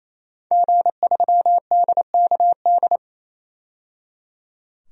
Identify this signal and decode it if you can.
Morse code, heard as a keyed tone, about 28 wpm, 700 Hz. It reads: G3DKD
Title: DXCC-100 Call Signs - 5 Characters - 28wpm